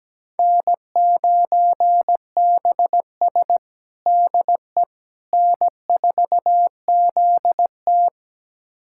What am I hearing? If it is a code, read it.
Morse code, heard as a keyed tone, about 17 wpm, 700 Hz. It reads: N9BS DE N4ZT
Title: USA Callsign DE Callsign - 4 Characters - 17wpm